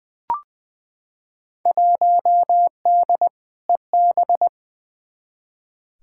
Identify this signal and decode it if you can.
Morse code, heard as a keyed tone, about 20 wpm, 700 Hz. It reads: 1D EB
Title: Field Day Exchanges - 20wpm